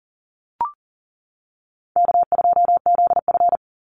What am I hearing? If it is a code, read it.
Morse code, heard as a keyed tone, about 40 wpm, 700 Hz. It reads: K2ZF
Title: Call Signs - 40wpm